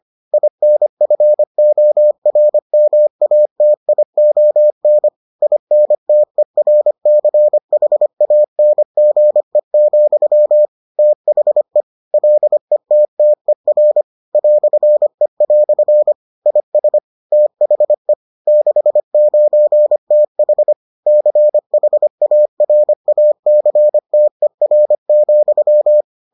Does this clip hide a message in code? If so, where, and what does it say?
Yes: Morse code from 0.3 to 26.0 s, heard as a keyed tone, about 25 wpm, 600 Hz. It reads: INFORMATION INTERCHANGE, THE LETTER "E" IS THE 69TH CHARACTER,